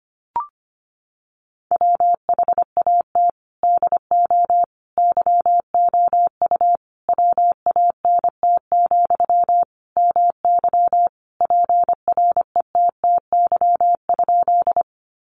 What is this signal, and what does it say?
Morse code, heard as a keyed tone, about 25 wpm, 700 Hz. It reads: WHAT DO YOU WANT, MY PRETTY?
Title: Sentences from Top 1000 Words 25wpm